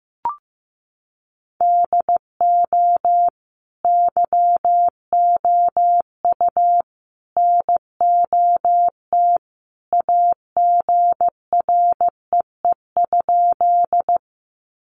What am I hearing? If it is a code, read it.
Morse code, heard as a keyed tone, about 15 wpm, 700 Hz. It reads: DO YOU NOT AGREE?
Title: Sentences from Top 1100 Words 15wpm